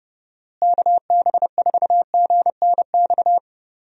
Morse code, heard as a keyed tone, about 32 wpm, 700 Hz. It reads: KB4GNX
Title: USA Call Signs - 6 Characters - Repeated 2x - 30wpm